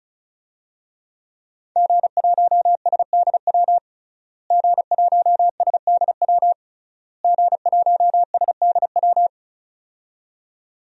Morse code, heard as a keyed tone, about 34 wpm, 700 Hz. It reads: G1SDW G1SDW G1SDW
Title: DXCC-100 Call Signs - 5 Characters - Repeated 3x - 35wpm